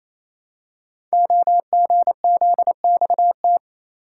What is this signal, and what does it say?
Morse code, heard as a keyed tone, about 29 wpm, 700 Hz. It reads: OGZXT
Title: Code Groups - 5 Letters - 28wpm